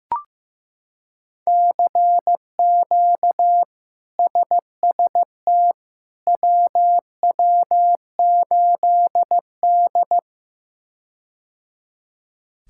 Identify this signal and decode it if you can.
Morse code, heard as a keyed tone, about 15 wpm, 700 Hz. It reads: CQ SST WW8D
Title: K1USN SST QSO Practice - 15wpm